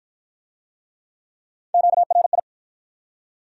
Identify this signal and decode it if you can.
Morse code, heard as a keyed tone, about 50 wpm, 700 Hz. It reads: QRI